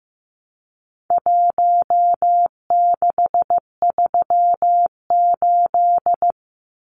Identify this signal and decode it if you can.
Morse code, heard as a keyed tone, about 15 wpm, 700 Hz. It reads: 1638
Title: Numbers - Contest Style - 4-Digit 15wpm